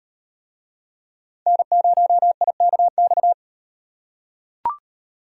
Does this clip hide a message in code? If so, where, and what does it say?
Yes: Morse code from 1.5 to 3.3 s, heard as a keyed tone, about 38 wpm, 700 Hz. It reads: N0IKX